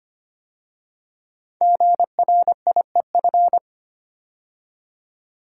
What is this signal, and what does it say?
Morse code, heard as a keyed tone, about 25 wpm, 700 Hz. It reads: GRIEF
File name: 5 Letter Words - First 3 Letters Given - 25wpm